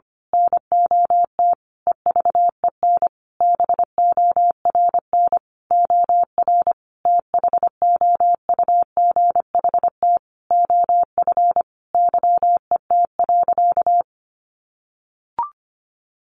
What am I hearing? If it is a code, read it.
Morse code, heard as a keyed tone, about 25 wpm, 700 Hz. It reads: NOT EVEN BORN OR THOUGHT OF YET.
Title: A Sound Of Thunder 25wpm